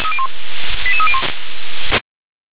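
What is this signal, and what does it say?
Five-tone selective call (ZVEI), heard as quick runs of five tones, tones of about 70 ms. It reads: 88301
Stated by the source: ZVEI five-tone call received by SDR.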